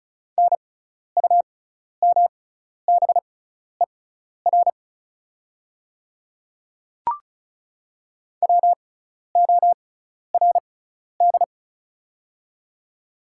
Morse code, heard as a keyed tone, about 36 wpm, spaced as for 12 wpm, 700 Hz. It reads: NUMBER WORD
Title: 6x Character Spacing - Top 200 Words - 35wpm